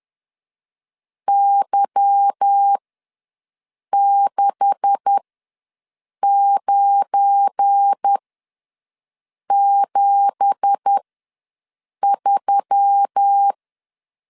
Morse code, heard as a keyed tone, about 10 wpm, spaced as for 6 wpm, 800 Hz. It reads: Y6973
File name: LEZIONI / 9 2867 ws 10wpm 4215